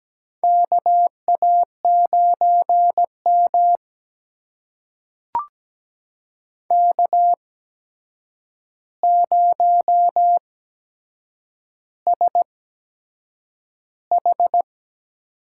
Morse code, heard as a keyed tone, about 17 wpm, spaced as for 7 wpm, 700 Hz. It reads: KA9M K0SH
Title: USA Call Signs - 4 Characters - 8x Character Spacing - 17wpm